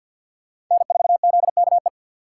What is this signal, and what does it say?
Morse code, heard as a keyed tone, about 52 wpm, 700 Hz. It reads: N4ZKE
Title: USA Call Signs - 3x Character Spacing - 50wpm